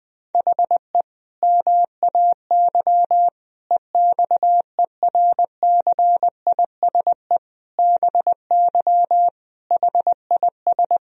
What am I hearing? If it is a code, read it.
Morse code, heard as a keyed tone, about 20 wpm, 700 Hz. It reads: HE MAY EXERCISE BY HIS
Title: Sentences from Top 1000 Words - Review Part 4 - 20wpm